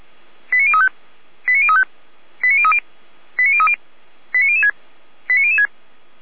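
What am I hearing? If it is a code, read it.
Five-tone selective call (ZVEI), heard as quick runs of five tones, tones of about 70 ms. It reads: 89035 89035 89030 89030 89006 89006
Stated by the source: ZVEI five-tone selective call received with an SDR.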